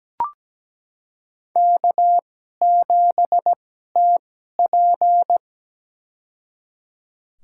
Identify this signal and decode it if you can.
Morse code, heard as a keyed tone, about 17 wpm, spaced as for 12 wpm, 700 Hz. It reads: K7TP
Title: USA Call Signs - 4 Characters - 2x Character Spacing - 17wpm